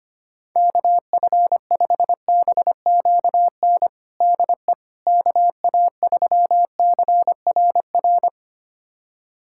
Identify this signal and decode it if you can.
Morse code, heard as a keyed tone, about 25 wpm, 700 Hz. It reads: KF5BQN DE KA3CRR
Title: USA Callsign DE Callsign - 6 Characters - 25wpm